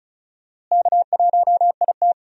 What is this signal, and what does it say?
Morse code, heard as a keyed tone, about 34 wpm, 700 Hz. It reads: K1IT
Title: USA Call Signs - 4 Characters - Repeated 2x - 35wpm